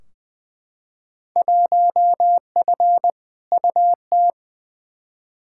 Morse code, heard as a keyed tone, about 20 wpm, 700 Hz. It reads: 1F UT
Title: Field Day Exchanges Encore - 20wpm